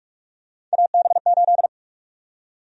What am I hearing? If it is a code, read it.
Morse code, heard as a keyed tone, about 46 wpm, 700 Hz. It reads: AB8
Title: USA Call Sign Prefixes - 45wpm